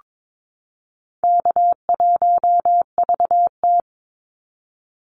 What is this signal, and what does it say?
Morse code, heard as a keyed tone, about 22 wpm, 700 Hz. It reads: K1VT
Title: Easy Call Signs Repeated 3x - 22wpm